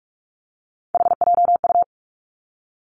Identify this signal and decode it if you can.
Morse code, heard as a keyed tone, about 46 wpm, 700 Hz. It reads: HJU
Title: Three Challenging Letter-Number Combinations 45wpm